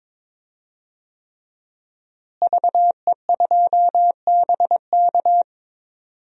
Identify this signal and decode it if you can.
Morse code, heard as a keyed tone, about 22 wpm, 700 Hz. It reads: VE2BK
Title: DXCC-100 Call Signs - 5 Characters - Repeated 3x - 22wpm